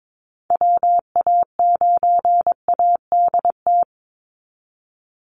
Morse code, heard as a keyed tone, about 22 wpm, 700 Hz. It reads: WA9ADT
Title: Call Signs Repeated 3x - 22wpm